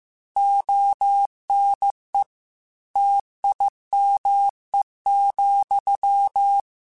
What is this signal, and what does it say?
Morse code, heard as a keyed tone, about 15 wpm, 800 Hz. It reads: ONE TIME,
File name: Technical Subjects / Technical Subjects-mp3s / AND Gate @13